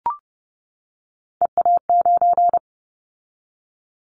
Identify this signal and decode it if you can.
Morse code, heard as a keyed tone, about 30 wpm, 700 Hz. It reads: EA9
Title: Callsign Prefixes for Top 100 DXCC - 30wpm